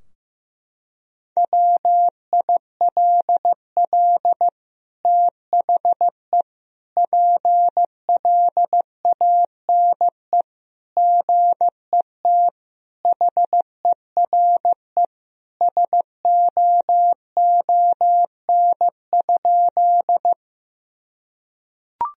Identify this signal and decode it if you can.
Morse code, heard as a keyed tone, about 15 wpm, 700 Hz. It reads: WILL THE PLANE GET HERE SOON?
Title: Sentences from Top 1000 Words - Review Part 3 - 15wpm